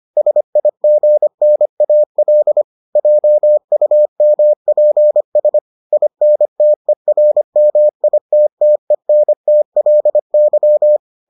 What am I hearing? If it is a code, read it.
Morse code, heard as a keyed tone, about 25 wpm, 600 Hz. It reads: SIGNAL JUMPS INTERMITTENTLY